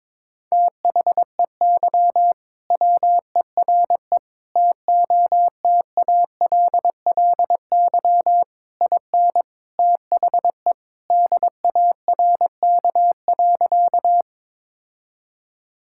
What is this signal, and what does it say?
Morse code, heard as a keyed tone, about 22 wpm, 700 Hz. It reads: THEY WERE TOTALLY IN THE DARK.
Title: Sentences from Top 1600 Words 22wpm